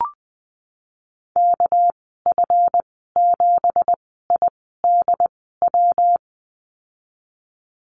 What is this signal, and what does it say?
Morse code, heard as a keyed tone, about 20 wpm, spaced as for 14 wpm, 700 Hz. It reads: KF7IDW
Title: 245 Lesson 39g - Letter z - Call Signs 20wpm